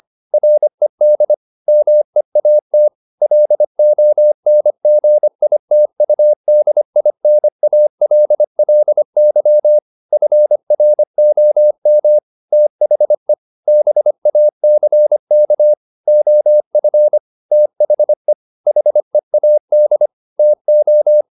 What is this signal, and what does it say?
Morse code, heard as a keyed tone, about 25 wpm, 600 Hz. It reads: RED MEAT LONGITUDINALLY FROM THE BACK OF THE HEAD TO
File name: the old man and the sea part 1 25wpm